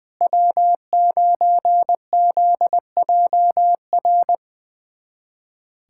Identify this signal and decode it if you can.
Morse code, heard as a keyed tone, about 21 wpm, 700 Hz. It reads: W9ZJR
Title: USA Callsign DE Callsign - 5 Characters - 20wpm